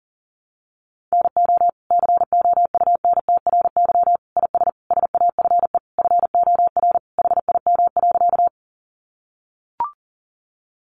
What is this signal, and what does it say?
Morse code, heard as a keyed tone, about 40 wpm, 700 Hz. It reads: NO COUNTRY IS SAFE FOR HIM.
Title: Sentences from Top 900 Words 40wpm